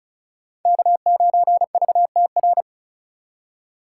Morse code, heard as a keyed tone, about 36 wpm, 700 Hz. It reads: K9VTR
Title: USA Call Signs - 5 Characters - 2x Character Spacing - 35wpm